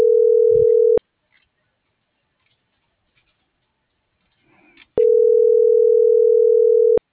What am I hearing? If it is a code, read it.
Ringback tone (North America), 440 + 480 Hz.